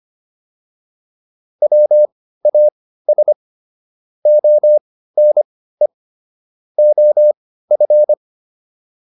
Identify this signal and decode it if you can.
Morse code, heard as a keyed tone, about 25 wpm, spaced as for 15 wpm, 600 Hz. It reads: WAS ONE OF